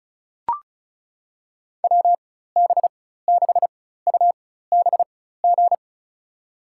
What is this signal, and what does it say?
Morse code, heard as a keyed tone, about 34 wpm, spaced as for 16 wpm, 700 Hz. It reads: WB6UBG
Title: USA Call Signs - 6 Characters - 4x Character Spacing - 35wpm